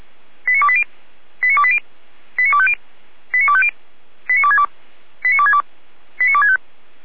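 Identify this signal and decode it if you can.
Five-tone selective call (ZVEI), heard as quick runs of five tones, tones of about 70 ms. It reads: 89280 89280 89250 89250 89262 89262 89265
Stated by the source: ZVEI five-tone call 89246 received on an SDR.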